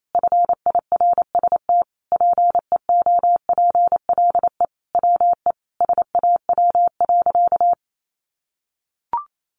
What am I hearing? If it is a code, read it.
Morse code, heard as a keyed tone, about 28 wpm, 700 Hz. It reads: FIRST PEOPLE WE SAW.
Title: Sentences from Top 300 Words 28wpm